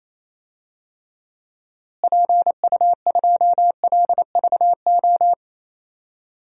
Morse code, heard as a keyed tone, about 27 wpm, 700 Hz. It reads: PU2LVO
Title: DXCC-100 Call Signs - 6 Characters - Repeated 3x - 28wpm